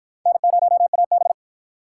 Morse code, heard as a keyed tone, about 48 wpm, 700 Hz. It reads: N0AB